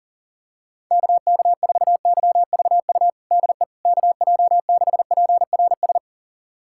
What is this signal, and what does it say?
Morse code, heard as a keyed tone, about 40 wpm, 700 Hz. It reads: KK4YVU DE KJ6PRS
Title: USA Callsign DE Callsign 40wpm